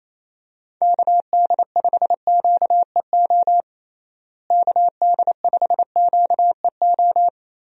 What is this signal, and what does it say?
Morse code, heard as a keyed tone, about 28 wpm, 700 Hz. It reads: KD5QEO KD5QEO
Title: USA Call Signs - 6 Characters - Repeated 3x - 28wpm